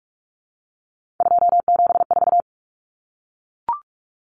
Morse code, heard as a keyed tone, about 44 wpm, 700 Hz. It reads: 274
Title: Numbers - 3-Digit 45wpm